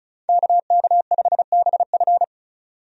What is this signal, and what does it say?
Morse code, heard as a keyed tone, about 34 wpm, 700 Hz. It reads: KK5BF
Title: USA Call Signs - 5 Characters - 35wpm